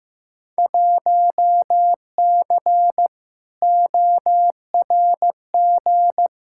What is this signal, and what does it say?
Morse code, heard as a keyed tone, about 15 wpm, 700 Hz. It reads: 1C ORG
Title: Field Day Exchanges - 15wpm